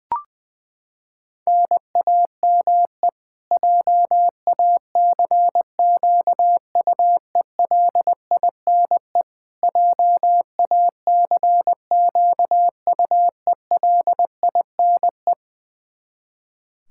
Morse code, heard as a keyed tone, about 21 wpm, 700 Hz. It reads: NAME JACQUELINE JACQUELINE
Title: K1USN SST QSO Practice 20wpm